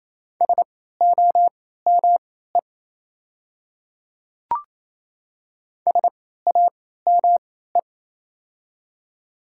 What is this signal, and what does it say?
Morse code, heard as a keyed tone, about 28 wpm, spaced as for 16 wpm, 700 Hz. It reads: SOME SAME
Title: 3x Character Spacing - Top 300 Words - 28wpm